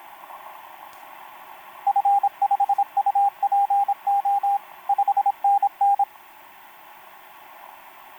Morse code, heard as a keyed tone, about 26 wpm, 800 Hz, with radio noise. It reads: F5UPO 5NN
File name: pj4-w9njy f5upo etc 12m cw 28mar2011 1924